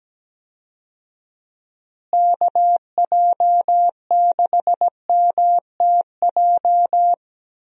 Morse code, heard as a keyed tone, about 17 wpm, 700 Hz. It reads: KJ6MTJ
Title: USA Call Signs - 6 Characters - Repeated 3x - 17wpm